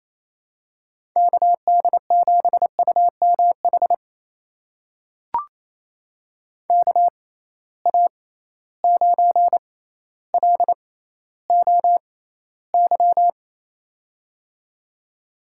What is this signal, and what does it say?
Morse code, heard as a keyed tone, about 28 wpm, spaced as for 14 wpm, 700 Hz. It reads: KD7UMH KA9LOY
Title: USA Call Signs - 6 Characters - 6x Character Spacing - 28wpm